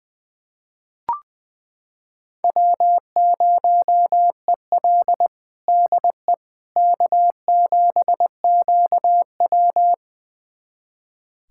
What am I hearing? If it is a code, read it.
Morse code, heard as a keyed tone, about 21 wpm, 700 Hz. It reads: W0EL DE K7QW
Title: USA Callsign DE Callsign - 4 Characters - 20wpm